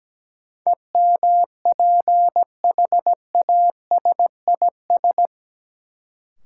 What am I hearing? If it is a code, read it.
Morse code, heard as a keyed tone, about 17 wpm, 700 Hz. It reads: EMPHASIS
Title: Sentences from Top 1400 Words - New Word Drill - 17wpm